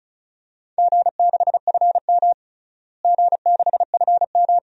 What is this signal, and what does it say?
Morse code, heard as a keyed tone, about 34 wpm, 700 Hz. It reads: G6FM G6FM
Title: DXCC-100 Call Signs - 4 Characters - Repeated 2x - 35wpm